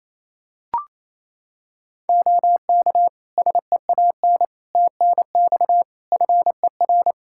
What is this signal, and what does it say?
Morse code, heard as a keyed tone, about 28 wpm, 700 Hz. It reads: OK SEAN TNX FER
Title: Practice QSO - Exchange 4 - Ending - 28wpm